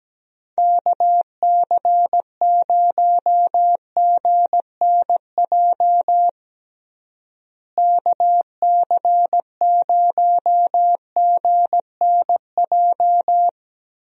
Morse code, heard as a keyed tone, about 17 wpm, 700 Hz. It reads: KC0GNJ KC0GNJ
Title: USA Call Signs - 6 Characters - Repeated 3x - 17wpm